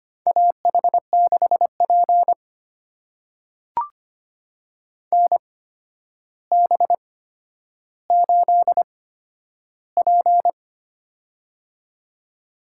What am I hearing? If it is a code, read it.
Morse code, heard as a keyed tone, about 25 wpm, spaced as for 11 wpm, 700 Hz. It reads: AH6P NB8P
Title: USA Call Signs - 4 Characters - 8x Character Spacing - 25wpm